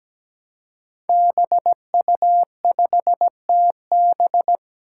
Morse code, heard as a keyed tone, about 17 wpm, 700 Hz. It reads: BU5TB
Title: Five Challenging Letter-Number Combinations 17wpm